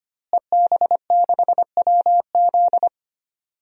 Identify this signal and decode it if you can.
Morse code, heard as a keyed tone, about 25 wpm, 700 Hz. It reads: EB6WZ
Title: DXCC-100 Call Signs - 5 Characters - Repeated 3x - 25wpm